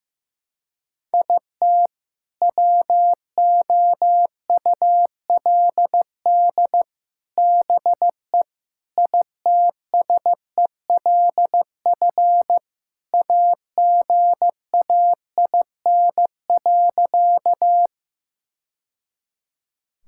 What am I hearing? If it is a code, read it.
Morse code, heard as a keyed tone, about 16 wpm, 700 Hz. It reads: IT WOULD BE ITSELF AGAIN.
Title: Sentences from Top 1000 Words - Review Part 5 - 15wpm